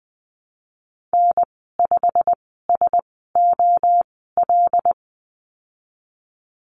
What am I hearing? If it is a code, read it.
Morse code, heard as a keyed tone, about 20 wpm, spaced as for 15 wpm, 700 Hz. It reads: N5SOL
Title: 088 Lesson 15g - Number 5 - Call Signs 20wpm